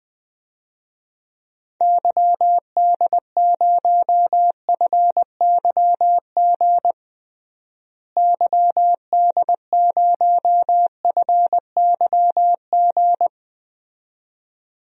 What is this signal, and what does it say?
Morse code, heard as a keyed tone, about 20 wpm, 700 Hz. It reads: YD0FYG YD0FYG
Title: DXCC-100 Call Signs - 6 Characters - Repeated 2x - 20wpm